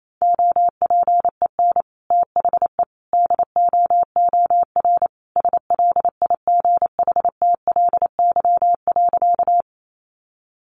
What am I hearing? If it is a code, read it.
Morse code, heard as a keyed tone, about 28 wpm, 700 Hz. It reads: OPEN THE DOOR SLIGHTLY.
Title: Sentences from Top 1300 Words 28wpm